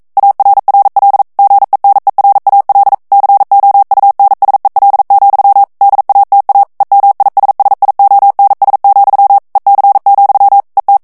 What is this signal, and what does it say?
Morse code, heard as a keyed tone, about 42 wpm, 800 Hz. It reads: ARRL GENERAL COUNSEL, DATA EMISSIONS, EC, ET
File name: 40 mots minute